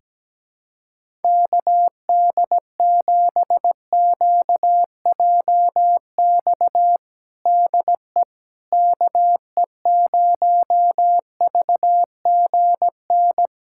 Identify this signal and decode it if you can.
Morse code, heard as a keyed tone, about 17 wpm, 700 Hz. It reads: KD7QJX DE KE0VGN
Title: USA Callsign DE Callsign 17wpm